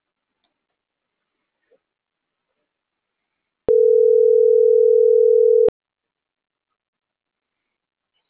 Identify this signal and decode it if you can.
Ringback tone (North America), 440 + 480 Hz.